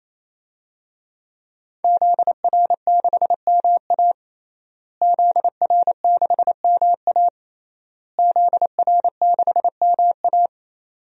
Morse code, heard as a keyed tone, about 28 wpm, 700 Hz. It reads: ZR6MA ZR6MA ZR6MA
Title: DXCC-100 Call Signs - 5 Characters - Repeated 3x - 28wpm